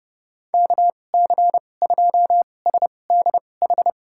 Morse code, heard as a keyed tone, about 31 wpm, spaced as for 22 wpm, 700 Hz. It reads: KC2SDH
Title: USA Call Signs - 6 Characters - 2x Character Spacing - 30wpm